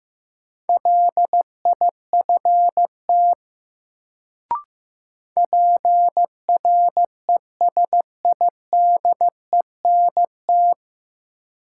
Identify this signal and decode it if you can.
Morse code, heard as a keyed tone, about 15 wpm, 700 Hz. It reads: LIFT PRESIDENT
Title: Sentences from Top 1000 Words - New Word Drill - 15wpm